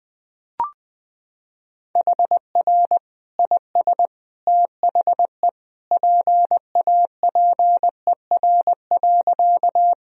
Morse code, heard as a keyed tone, about 21 wpm, 700 Hz. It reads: HR IS THE PAPER.
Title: Sentences from Top 300 Words and Abbreviations 20wpm